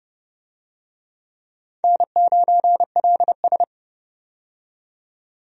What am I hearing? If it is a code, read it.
Morse code, heard as a keyed tone, about 30 wpm, 700 Hz. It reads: N9LS
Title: USA Call Signs - 4 Characters - 30wpm